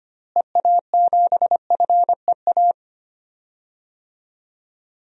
Morse code, heard as a keyed tone, about 25 wpm, 700 Hz. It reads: EA7FEA
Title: DXCC-100 Call Signs - 6 Characters - Repeated 3x - 25wpm